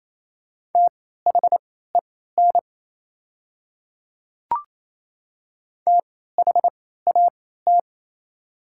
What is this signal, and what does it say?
Morse code, heard as a keyed tone, about 27 wpm, spaced as for 16 wpm, 690 Hz. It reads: THEN THAT
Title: 3x Character Spacing - Top 100 Words - 28wpm